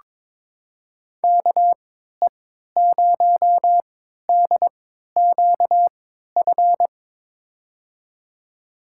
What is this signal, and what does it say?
Morse code, heard as a keyed tone, about 22 wpm, spaced as for 13 wpm, 700 Hz. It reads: KE0DQF
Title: USA Call Signs - 6 Characters - 3x Character Spacing - 22wpm